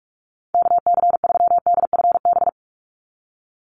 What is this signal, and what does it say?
Morse code, heard as a keyed tone, about 44 wpm, 700 Hz. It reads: KC3DFB
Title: USA Call Signs 45wpm